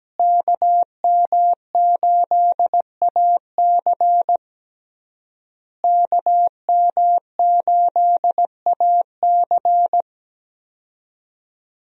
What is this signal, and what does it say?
Morse code, heard as a keyed tone, about 17 wpm, 700 Hz. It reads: KM8AC KM8AC
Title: USA Call Signs - 5 Characters - Repeated 3x - 17wpm